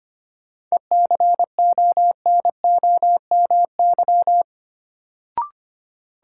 Morse code, heard as a keyed tone, about 25 wpm, 700 Hz. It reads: ECONOMY
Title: Sentences from Top 1200 Words - New Word Drill Alt - 25wpm